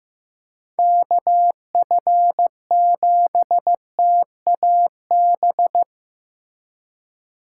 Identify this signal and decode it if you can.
Morse code, heard as a keyed tone, about 15 wpm, 700 Hz. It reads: KF7TAB
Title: USA Call Signs - 6 Characters - 4x Character Spacing - 15wpm